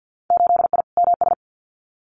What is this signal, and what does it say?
Morse code, heard as a keyed tone, about 50 wpm, 700 Hz. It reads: 8I MS
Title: Winter Field Day Exchanges Encore 50wpm